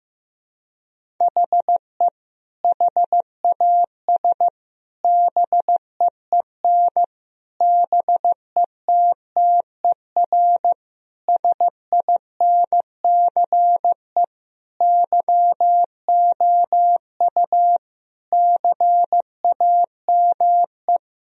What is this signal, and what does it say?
Morse code, heard as a keyed tone, about 16 wpm, 700 Hz. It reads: HE HAS BEEN BETTER SINCE YOU CAME
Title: Sentences from Top 1000 Words - Review Part 1 - 15wpm